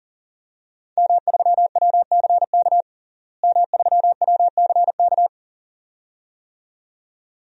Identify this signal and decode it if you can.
Morse code, heard as a keyed tone, about 37 wpm, 700 Hz. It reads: M3WCK M3WCK
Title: DXCC-100 Call Signs - 5 Characters - Repeated 2x - 40wpm